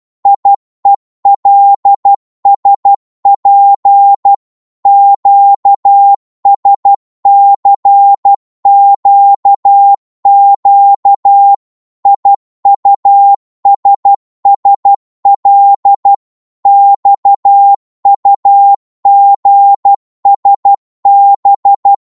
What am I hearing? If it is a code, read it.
Morse code, heard as a keyed tone, about 12 wpm, 800 Hz. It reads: IELSP QSCQQ IUSSL XUGSB
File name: Morse / ltr-12wpm